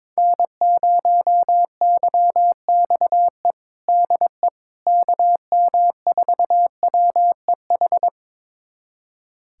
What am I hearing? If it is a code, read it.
Morse code, heard as a keyed tone, about 22 wpm, 700 Hz. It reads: N0YXE DE KM4WEH
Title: USA Callsign DE Callsign 22wpm